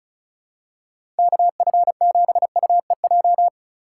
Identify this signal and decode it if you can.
Morse code, heard as a keyed tone, about 36 wpm, 700 Hz. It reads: KF7UEJ
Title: USA Call Signs - 6 Characters - 3x Character Spacing - 35wpm